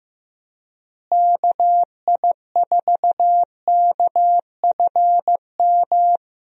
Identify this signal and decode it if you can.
Morse code, heard as a keyed tone, about 15 wpm, 700 Hz. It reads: KI4KFM
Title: USA Call Signs - 6 Characters - Repeated 3x - 15wpm